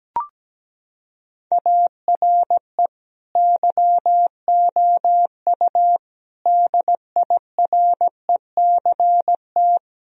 Morse code, heard as a keyed tone, about 17 wpm, 700 Hz. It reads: ARE YOU DIRECT
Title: Sentences from Top 1000 Words - Review Part 4 - 17wpm